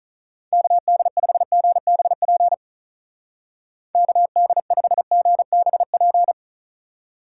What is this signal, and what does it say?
Morse code, heard as a keyed tone, about 45 wpm, 700 Hz. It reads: KD5GBP KD5GBP